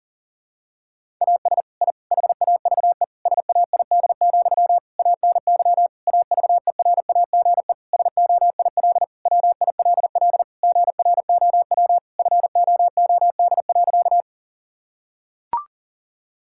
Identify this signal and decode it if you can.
Morse code, heard as a keyed tone, about 40 wpm, 700 Hz. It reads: AS I HAVE SAID, ANY AVERAGE SOIL WILL GROW FOOD.
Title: Sentences from Top 1000 Words - Review Part 6 - 40wpm